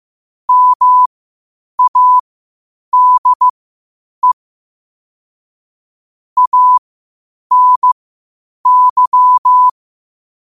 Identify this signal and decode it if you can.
Morse code, heard as a keyed tone, about 15 wpm, spaced as for 8 wpm, 1000 Hz. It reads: MADE ANY